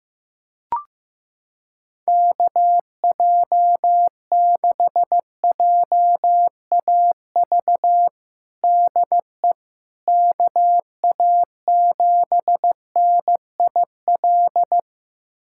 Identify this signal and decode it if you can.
Morse code, heard as a keyed tone, about 15 wpm, 700 Hz. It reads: KJ6JAV DE KA7NIL
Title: USA Callsign DE Callsign 15wpm